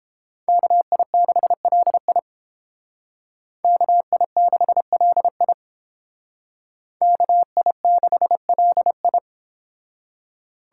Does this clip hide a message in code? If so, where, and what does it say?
Yes: Morse code from 0.5 to 9.2 s, heard as a keyed tone, about 29 wpm, 700 Hz. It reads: KI6LI KI6LI KI6LI